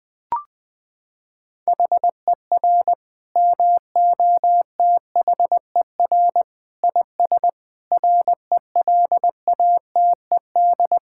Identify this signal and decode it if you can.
Morse code, heard as a keyed tone, about 20 wpm, 700 Hz. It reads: HER MOTHER IS RELATED
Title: Sentences from Top 1000 Words - Review Part 5 - 20wpm